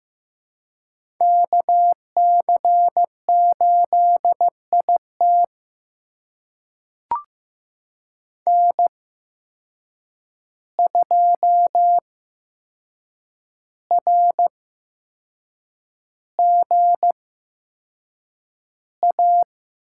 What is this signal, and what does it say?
Morse code, heard as a keyed tone, about 15 wpm, spaced as for 6 wpm, 700 Hz. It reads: KC8IT N2RGA
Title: USA Call Signs - 5 Characters - 8x Character Spacing - 15wpm